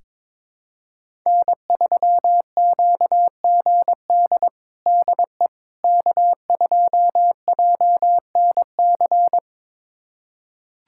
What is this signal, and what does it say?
Morse code, heard as a keyed tone, about 21 wpm, 700 Hz. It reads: N3QGD DE K2JNC
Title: USA Callsign DE Callsign - 5 Characters - 22wpm